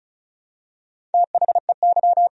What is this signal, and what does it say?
Morse code, heard as a keyed tone, about 36 wpm, 700 Hz. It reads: THEY
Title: Sentences from Top 400 Words and Abbreviations 35wpm